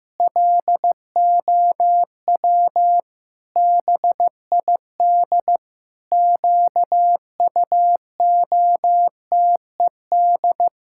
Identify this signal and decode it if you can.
Morse code, heard as a keyed tone, about 15 wpm, 700 Hz. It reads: LOW BID QUOTED
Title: Pangram Challenge v2 - 15wpm